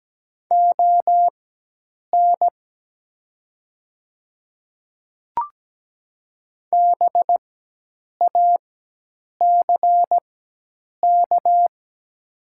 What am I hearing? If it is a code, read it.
Morse code, heard as a keyed tone, about 17 wpm, spaced as for 8 wpm, 700 Hz. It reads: ON BACK
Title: 4x Character Spacing - Top 300 Words - 17wpm